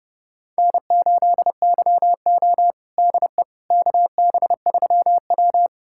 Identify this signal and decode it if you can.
Morse code, heard as a keyed tone, about 30 wpm, 700 Hz. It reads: N8YO DE KB3W
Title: USA Callsign DE Callsign - 4 Characters - 30wpm